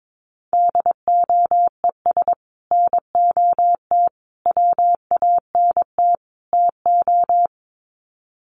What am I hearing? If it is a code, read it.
Morse code, heard as a keyed tone, about 22 wpm, 700 Hz. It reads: DOES NOT WANT TO
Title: Sets of 4 Words 22wpm